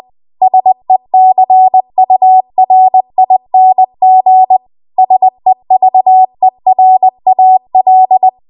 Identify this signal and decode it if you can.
Morse code, heard as a keyed tone, about 20 wpm, 750 Hz. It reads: SECURING SEVERAL